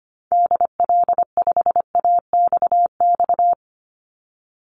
Morse code, heard as a keyed tone, about 25 wpm, 700 Hz. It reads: DL5AXX
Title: Call Signs - 25wpm